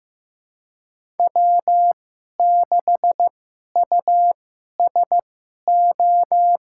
Morse code, heard as a keyed tone, about 15 wpm, spaced as for 11 wpm, 700 Hz. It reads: W6USO
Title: USA Call Signs - 2x Character Spacing - 15wpm